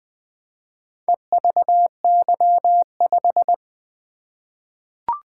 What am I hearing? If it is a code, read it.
Morse code, heard as a keyed tone, about 20 wpm, 700 Hz. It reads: EVY5
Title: Four Letter-Number Combinations - 20wpm